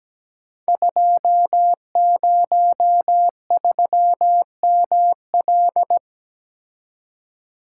Morse code, heard as a keyed tone, about 17 wpm, 700 Hz. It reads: 203ML
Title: Code Groups - 5 Letter:Numbers - 17wpm